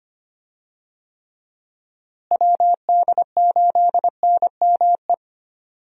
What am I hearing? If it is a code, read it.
Morse code, heard as a keyed tone, about 25 wpm, 700 Hz. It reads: WD8NME
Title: USA Call Signs - 6 Characters - Repeated 2x - 25wpm